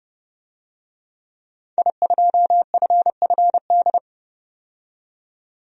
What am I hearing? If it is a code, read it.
Morse code, heard as a keyed tone, about 30 wpm, 700 Hz. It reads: I2FFD
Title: DXCC-100 Call Signs - 5 Characters - 30wpm